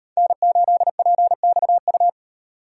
Morse code, heard as a keyed tone, about 38 wpm, 700 Hz. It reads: N8PXU